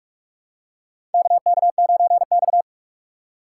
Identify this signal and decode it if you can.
Morse code, heard as a keyed tone, about 45 wpm, 700 Hz. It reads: KK9X
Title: USA Call Signs - 4 Characters - 6x Character Spacing - 45wpm